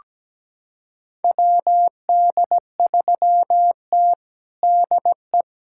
Morse code, heard as a keyed tone, about 17 wpm, 700 Hz. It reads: WD3T DE
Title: USA Callsign DE Callsign - 4 Characters - 17wpm